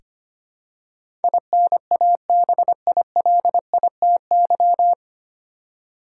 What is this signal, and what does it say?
Morse code, heard as a keyed tone, about 25 wpm, 700 Hz. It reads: INABILITY
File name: practice / 25wpm / 9 Letter Words - First 3 Letters Given - 25wpm